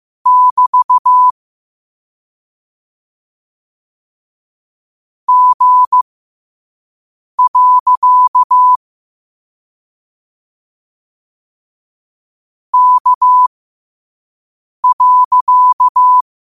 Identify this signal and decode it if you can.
Morse code, heard as a keyed tone, about 15 wpm, spaced as for 5 wpm, 1000 Hz. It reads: <BT> G. K.